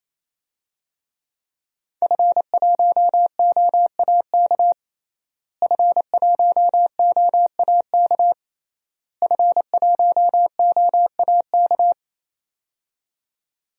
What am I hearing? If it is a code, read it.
Morse code, heard as a keyed tone, about 27 wpm, 700 Hz. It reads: F1OAK F1OAK F1OAK
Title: DXCC-100 Call Signs - 5 Characters - Repeated 3x - 28wpm